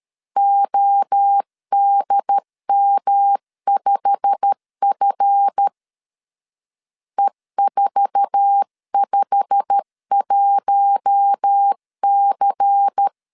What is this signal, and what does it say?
Morse code, heard as a keyed tone, about 13 wpm, 800 Hz. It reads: ODM5F E451C
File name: LEZIONI / 8 YJ45 / 8 YJ45 12wpm 2151